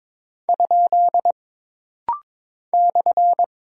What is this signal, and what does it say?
Morse code, heard as a keyed tone, about 22 wpm, 700 Hz. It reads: ? /
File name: practice / 22wpm / Single Punctuation-Prosign - Rapid-Fire - 22wpm